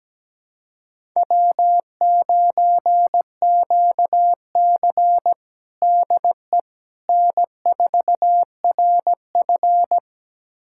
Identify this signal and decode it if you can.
Morse code, heard as a keyed tone, about 17 wpm, 700 Hz. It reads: W9QC DE N4RF
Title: USA Callsign DE Callsign - 4 Characters - 17wpm